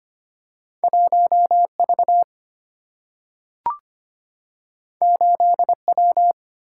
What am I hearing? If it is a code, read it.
Morse code, heard as a keyed tone, about 25 wpm, 700 Hz. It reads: 1V 8W
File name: Two Letter-Number Combinations - 25wpm